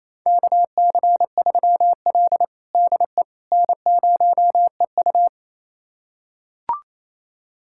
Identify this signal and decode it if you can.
Morse code, heard as a keyed tone, about 28 wpm, 700 Hz. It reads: KC3L DE N0EU
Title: USA Callsign DE Callsign - 4 Characters - 28wpm